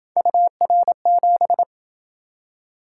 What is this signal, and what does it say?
Morse code, heard as a keyed tone, about 27 wpm, 700 Hz. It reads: UR7